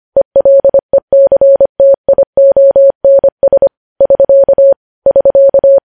Morse code, heard as a keyed tone, about 25 wpm, 550 Hz. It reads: ELECTIONS <SK> <SK>